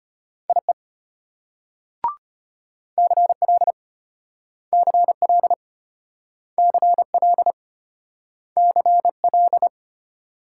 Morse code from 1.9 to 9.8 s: CL CL CL CL

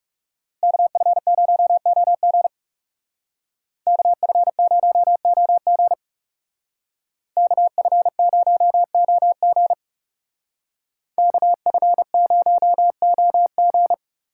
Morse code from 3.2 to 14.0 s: KF0OG KF0OG KF0OG

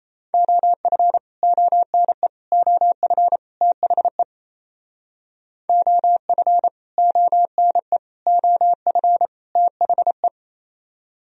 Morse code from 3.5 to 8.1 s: THE OF ONE

